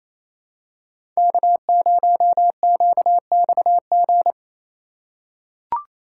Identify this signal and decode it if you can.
Morse code, heard as a keyed tone, about 28 wpm, 700 Hz. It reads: K0QXG
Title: USA Call Signs - 8x Character Spacing - 28wpm